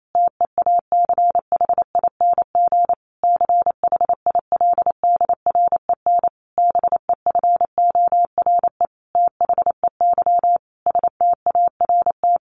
Morse code, heard as a keyed tone, about 28 wpm, 700 Hz. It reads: TEACHING CHILDREN BEFORE THEY START
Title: Sentences from Top 1100 Words 28wpm